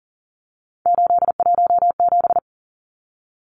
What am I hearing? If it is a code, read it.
Morse code, heard as a keyed tone, about 40 wpm, 700 Hz. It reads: 817
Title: Numbers - 3-Digit 40wpm